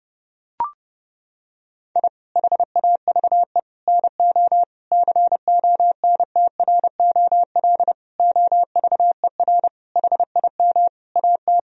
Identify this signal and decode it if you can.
Morse code, heard as a keyed tone, about 30 wpm, 700 Hz. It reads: I HAVE NO CONTROL OVER HIM AT
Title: Sentences from Top 1000 Words - Review Part 3 - 30wpm